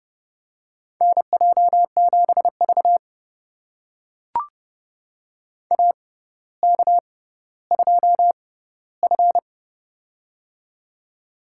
Morse code, heard as a keyed tone, about 30 wpm, spaced as for 15 wpm, 700 Hz. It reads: NJ7V AK2F
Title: USA Call Signs - 4 Characters - 6x Character Spacing - 30wpm